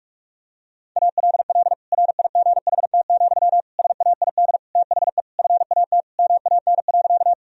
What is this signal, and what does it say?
Morse code, heard as a keyed tone, about 46 wpm, 700 Hz. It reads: ALL RIGHT, SAID THE FAT MAN.
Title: Sentences from Top 1000 Words - Review Part 4 - 45wpm